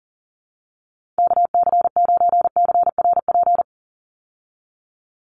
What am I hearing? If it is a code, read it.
Morse code, heard as a keyed tone, about 37 wpm, 700 Hz. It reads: KC9CRP
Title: USA Call Signs 40wpm